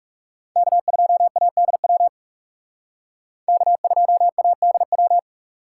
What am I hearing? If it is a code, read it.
Morse code, heard as a keyed tone, about 38 wpm, 700 Hz. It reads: K2ADW K2ADW